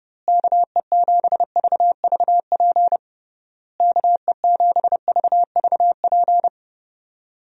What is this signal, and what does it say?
Morse code, heard as a keyed tone, about 31 wpm, 700 Hz. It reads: KE7VVP KE7VVP
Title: USA Call Signs - 6 Characters - Repeated 3x - 30wpm